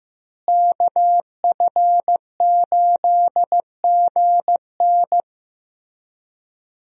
Morse code, heard as a keyed tone, about 15 wpm, 700 Hz. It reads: KF8GN
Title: USA Call Signs - 5 Characters - Repeated 2x - 15wpm